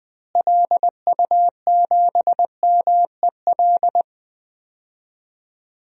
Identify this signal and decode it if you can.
Morse code, heard as a keyed tone, about 20 wpm, 700 Hz. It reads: LU7MEL
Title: DXCC-100 Call Signs - 6 Characters - Repeated 3x - 20wpm